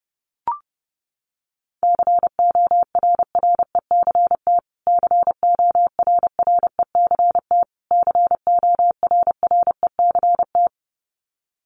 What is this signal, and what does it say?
Morse code, heard as a keyed tone, about 30 wpm, 700 Hz. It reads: CORRECT CORRECT CORRECT
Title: Top 500 Words Repeated 30wpm